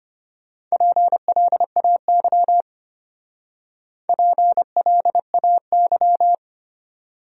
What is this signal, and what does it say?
Morse code, heard as a keyed tone, about 29 wpm, 700 Hz. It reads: PLAY PLAY
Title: Speed-Racing Top 300 Words 25wpm